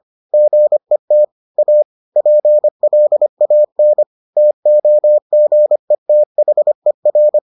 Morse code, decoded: GET A PLAN TOGETHER

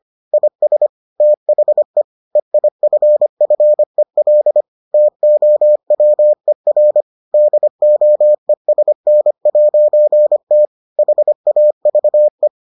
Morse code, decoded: IS THE EIFFEL TOWER DOESN'T HAVE

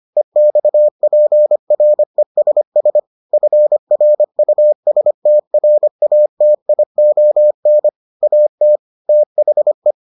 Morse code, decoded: EXPRESS FRUSTRATION AT THE

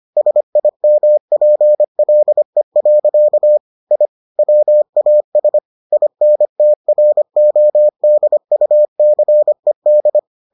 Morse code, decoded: SIMPLE. I WAS INTRODUCED

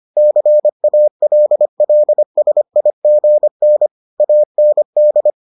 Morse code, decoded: CALLSIGN AND

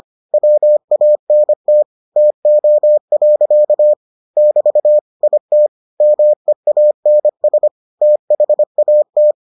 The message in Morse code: WANT TO. <BT> IT MEANS THAT

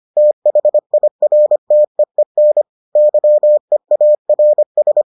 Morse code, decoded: THIRTEEN YEARS

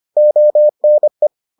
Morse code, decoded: ONE

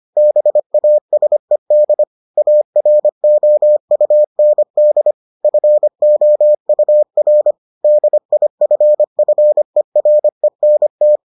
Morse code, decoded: BASED AROUND FOUR DIFFERENT

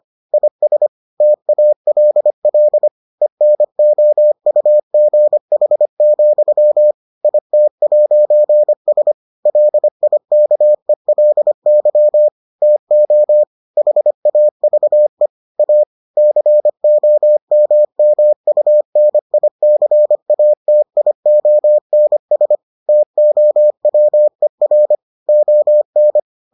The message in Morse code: IS TALL ENOUGH, IT'S LIKELY TO HAVE A COMMUNICATIONS TOWER ON

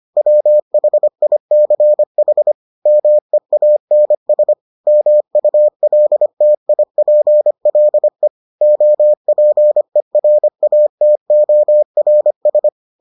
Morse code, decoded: WHICH MEANS MULTIPLE OPERATORS

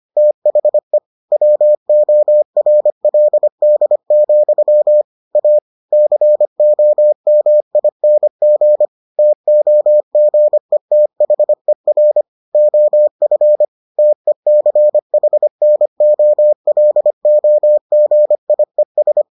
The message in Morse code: THE WORLD, A COMING TOGETHER OF TECHNOLOGIES

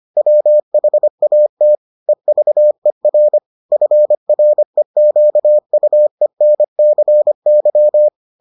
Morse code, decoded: WHAT EVER FREQUENCY

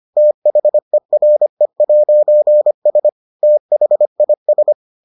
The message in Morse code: THERE'S THIS

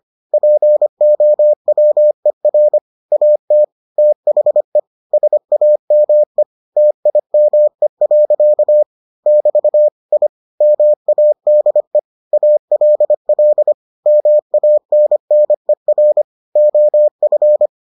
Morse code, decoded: POWER AT THE SAME TIME. <BT> I MADE ALL MANNER OF

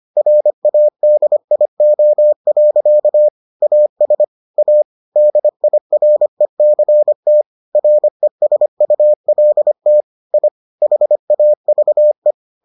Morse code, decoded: RADIO. AS A DIRECT RESULT I HAVE

